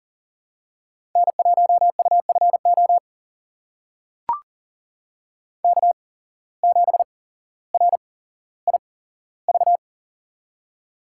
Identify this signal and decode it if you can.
Morse code, heard as a keyed tone, about 40 wpm, spaced as for 17 wpm, 700 Hz. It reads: N1UFO K7RIV
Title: USA Call Signs - 8x Character Spacing - 40wpm